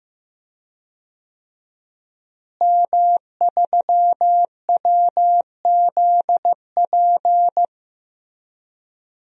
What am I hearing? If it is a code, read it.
Morse code, heard as a keyed tone, about 15 wpm, 700 Hz. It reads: M3WZP
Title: DXCC-100 Call Signs - 5 Characters - Repeated 2x - 15wpm